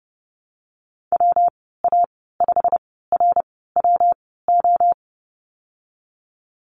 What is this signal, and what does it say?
Morse code, heard as a keyed tone, about 30 wpm, 700 Hz. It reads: W A 5 R W O
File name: Lesson 19g - Letter w - Call Signs 30wpm